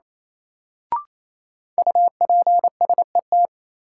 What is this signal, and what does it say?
Morse code, heard as a keyed tone, about 28 wpm, 700 Hz. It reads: UPSET
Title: Sentences from Top 800 Words - New Word Drill - 28wpm